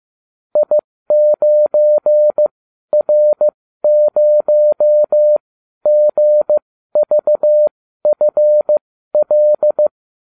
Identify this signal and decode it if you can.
Morse code, heard as a keyed tone, about 15 wpm, spaced as for 12 wpm, 600 Hz. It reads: I9R0GVFL